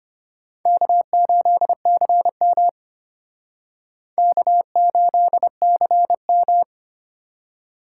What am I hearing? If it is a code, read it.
Morse code, heard as a keyed tone, about 31 wpm, 700 Hz. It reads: K8CM K8CM
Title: Speed Racing - USA 1x2 Call Signs 25wpm